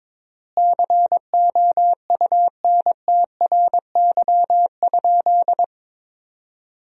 Morse code, heard as a keyed tone, about 21 wpm, 700 Hz. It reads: COUNTRY?
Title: 3x Word-Spacing - Sentences from Top 300 Words - 22wpm